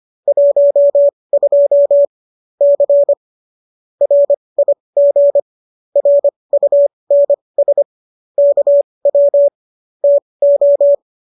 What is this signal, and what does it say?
Morse code, heard as a keyed tone, about 25 wpm, spaced as for 20 wpm, 570 Hz. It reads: 12 C RIG RUNS KW TO